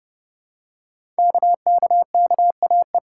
Morse code, heard as a keyed tone, about 30 wpm, 700 Hz. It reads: KKKAE
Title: Code Groups - 5 Letters - 30wpm